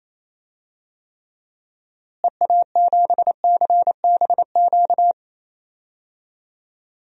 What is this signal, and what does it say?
Morse code, heard as a keyed tone, about 27 wpm, 700 Hz. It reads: EA7CBQ
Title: DXCC-100 Call Signs - 6 Characters - Repeated 3x - 28wpm